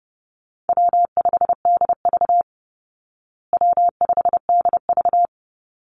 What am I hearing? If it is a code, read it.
Morse code, heard as a keyed tone, about 30 wpm, 700 Hz. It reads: W5DV W5DV
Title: Call Signs Repeated 3x - 30wpm